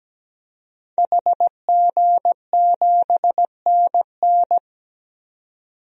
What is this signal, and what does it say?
Morse code, heard as a keyed tone, about 17 wpm, 700 Hz. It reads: HG7NN
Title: DXCC-100 Call Signs - 5 Characters - 17wpm